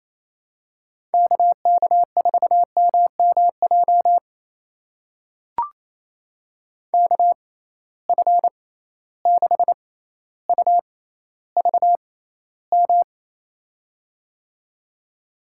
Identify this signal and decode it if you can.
Morse code, heard as a keyed tone, about 28 wpm, spaced as for 14 wpm, 700 Hz. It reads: KK4MMJ KF6UVM
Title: USA Call Signs - 6 Characters - 6x Character Spacing - 28wpm